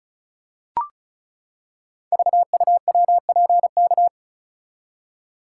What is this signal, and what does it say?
Morse code, heard as a keyed tone, about 36 wpm, 690 Hz. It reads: VUWPK
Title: Code Groups - 5 Letters - 35wpm